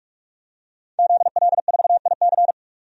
Morse code, heard as a keyed tone, about 43 wpm, 700 Hz. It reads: ZL4IC
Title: DXCC-100 Call Signs - 5 Characters - 45wpm